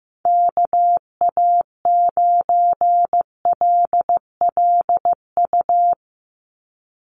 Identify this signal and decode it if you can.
Morse code, heard as a keyed tone, about 15 wpm, 700 Hz. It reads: KA9LLU
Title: USA Call Signs 15wpm